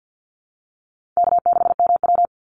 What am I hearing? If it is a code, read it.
Morse code, heard as a keyed tone, about 50 wpm, 700 Hz. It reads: K6MW
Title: NA SOTA Chaser Callsigns 50wpm